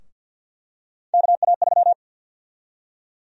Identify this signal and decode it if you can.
Morse code, heard as a keyed tone, about 47 wpm, 700 Hz. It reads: KA3
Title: USA Call Sign Prefixes - Repeated 2x - 50wpm